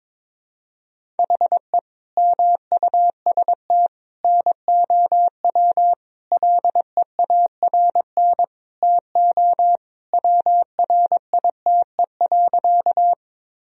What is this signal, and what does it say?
Morse code, heard as a keyed tone, about 22 wpm, 700 Hz. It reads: HE MUST NOW LEARN TO WRITE.
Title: Sentences from Top 1000 Words - Review Part 1 - 22wpm